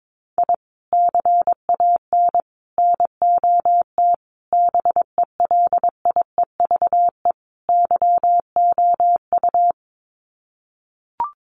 Morse code, heard as a keyed tone, about 22 wpm, 700 Hz. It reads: I CAN NOT BELIEVE YOU
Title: Sets of 5 Words Encore 22wpm